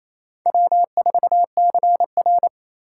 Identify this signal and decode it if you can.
Morse code, heard as a keyed tone, about 28 wpm, 700 Hz. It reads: W4CR
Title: USA Call Signs - 4 Characters - 8x Character Spacing - 28wpm